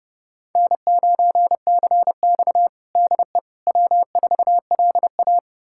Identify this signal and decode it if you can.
Morse code, heard as a keyed tone, about 30 wpm, 700 Hz. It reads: N9CX DE W4LA
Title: USA Callsign DE Callsign - 4 Characters - 30wpm